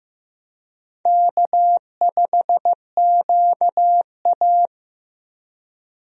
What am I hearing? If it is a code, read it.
Morse code, heard as a keyed tone, about 15 wpm, 700 Hz. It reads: K5QA
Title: USA Call Signs - 4 Characters - 15wpm